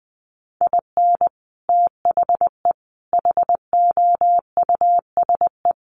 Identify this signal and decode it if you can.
Morse code, heard as a keyed tone, about 20 wpm, 700 Hz. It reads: IN THE HOUSE
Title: Sets of 3 Words 20wpm